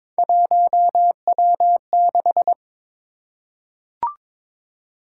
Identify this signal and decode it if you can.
Morse code, heard as a keyed tone, about 22 wpm, 700 Hz. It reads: 1W6
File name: practice / 22wpm / Three Letter-Number Combinations - 22wpm